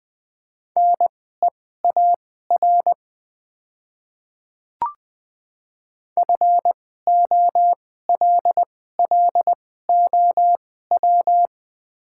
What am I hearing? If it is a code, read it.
Morse code, heard as a keyed tone, about 21 wpm, spaced as for 15 wpm, 700 Hz. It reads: NEAR FOLLOW
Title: 2x Character Spacing - Top 300 Words - 20wpm